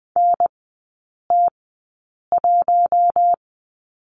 Morse code, heard as a keyed tone, about 20 wpm, spaced as for 8 wpm, 700 Hz. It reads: NT1
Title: 074 Lesson 13e - Letter L - Three Characters 20wpm